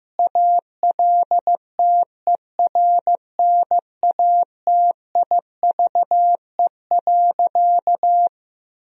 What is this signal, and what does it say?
Morse code, heard as a keyed tone, about 15 wpm, 700 Hz. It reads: ALTERNATIVE.
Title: Sentences from Top 1000 Words - Review Part 4 - 15wpm